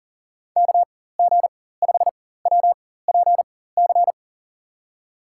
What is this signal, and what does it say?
Morse code, heard as a keyed tone, about 40 wpm, spaced as for 19 wpm, 700 Hz. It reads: KG5WPC
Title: USA Call Signs - 4x Character Spacing - 40wpm